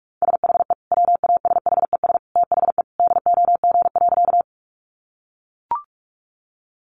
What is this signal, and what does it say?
Morse code, heard as a keyed tone, about 44 wpm, 700 Hz. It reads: SHE WASHES THE DOG.
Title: Easy Phrases 45wpm